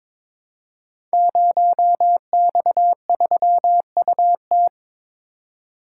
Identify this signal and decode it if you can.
Morse code, heard as a keyed tone, about 22 wpm, 700 Hz. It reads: 0X3UT
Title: Code Groups - 5 Letter:Numbers - 22wpm